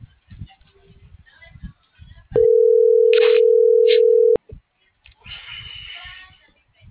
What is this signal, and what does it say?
Ringback tone (North America), 440 + 480 Hz.
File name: CENTRAL FRALSE / out gerardo2024sub